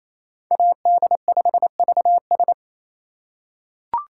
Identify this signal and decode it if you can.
Morse code, heard as a keyed tone, about 28 wpm, 700 Hz. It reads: AD5VS
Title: USA Call Signs - 2x Character Spacing - 28wpm